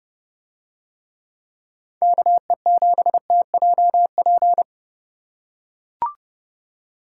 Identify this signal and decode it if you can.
Morse code, heard as a keyed tone, about 30 wpm, 700 Hz. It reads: KE7TJP